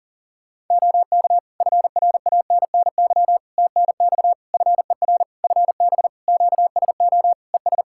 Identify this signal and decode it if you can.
Morse code, heard as a keyed tone, about 40 wpm, 700 Hz. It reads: OK FRANNY TNX FER FB QSO ES
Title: Practice QSO - Exchange 4 - Ending - 40wpm